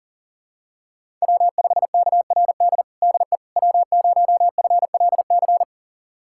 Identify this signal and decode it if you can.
Morse code, heard as a keyed tone, about 40 wpm, 700 Hz. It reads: W5KRD DE W0FLC
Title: USA Callsign DE Callsign - 5 Characters - 40wpm